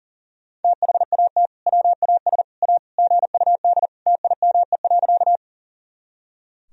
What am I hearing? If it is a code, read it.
Morse code, heard as a keyed tone, about 40 wpm, 700 Hz. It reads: THAT WAS A GUD TIME.
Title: Sentences from Top 200 Words and Abbreviations 40wpm